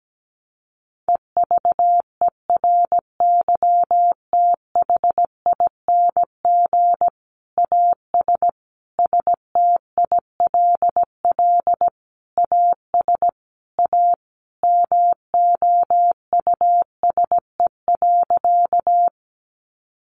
Morse code, heard as a keyed tone, about 17 wpm, 700 Hz. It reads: EVERYTHING AS STILL AS A MOUSE.
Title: Sentences from Top 700 Words 17wpm